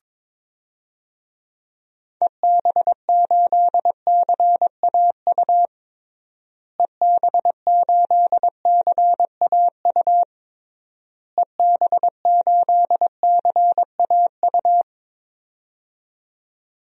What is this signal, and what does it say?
Morse code, heard as a keyed tone, about 22 wpm, 700 Hz. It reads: EB8CAU EB8CAU EB8CAU
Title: DXCC-100 Call Signs - 6 Characters - Repeated 3x - 22wpm